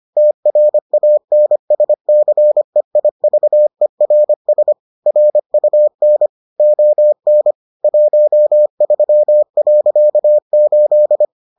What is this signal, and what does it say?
Morse code, heard as a keyed tone, about 25 wpm, 600 Hz. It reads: TRANSCEIVERS RUN ON 13.8